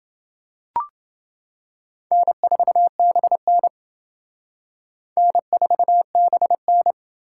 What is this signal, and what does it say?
Morse code, heard as a keyed tone, about 28 wpm, 700 Hz. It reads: N4BN N4BN